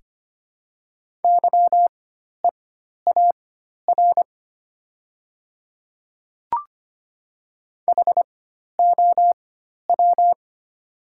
Morse code, heard as a keyed tone, about 25 wpm, spaced as for 12 wpm, 700 Hz. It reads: YEAR HOW
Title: 4x Character Spacing - Top 200 Words - 25wpm